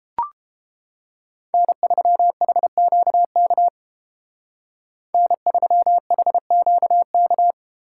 Morse code, heard as a keyed tone, about 29 wpm, 700 Hz. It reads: N3HQK N3HQK